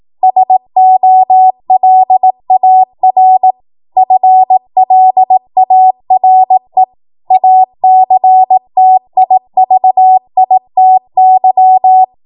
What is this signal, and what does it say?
Morse code, heard as a keyed tone, about 18 wpm, 750 Hz. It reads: SOLAR FLARE ACTIVITY
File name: Morse / 18WPM